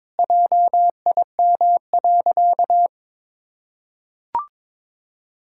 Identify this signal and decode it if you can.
Morse code, heard as a keyed tone, about 22 wpm, 700 Hz. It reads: JIM.
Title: Pangram Challenge v2 - 22wpm